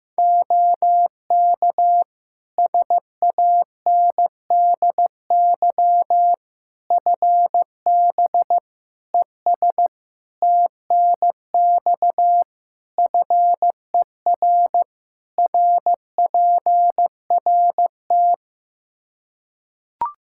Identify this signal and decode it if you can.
Morse code, heard as a keyed tone, about 15 wpm, 700 Hz. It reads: OK SANDY FB ES TNX FER RPRT
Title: K1USN SST QSO Practice 15wpm